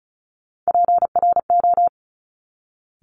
Morse code, heard as a keyed tone, about 36 wpm, 700 Hz. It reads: PRO
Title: Top Three Letter Combinations in Words 35wpm